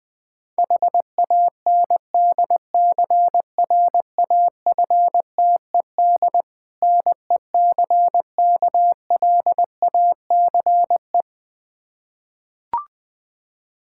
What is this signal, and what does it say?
Morse code, heard as a keyed tone, about 20 wpm, 700 Hz. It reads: HANDCRAFTED NECKLACE
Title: Difficult Words with Hints - 20wpm